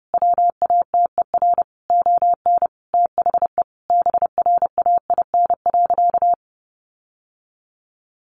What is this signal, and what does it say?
Morse code, heard as a keyed tone, about 30 wpm, 700 Hz. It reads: WATER ON THE BRAIN.
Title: Sentences from Top 1100 Words 30wpm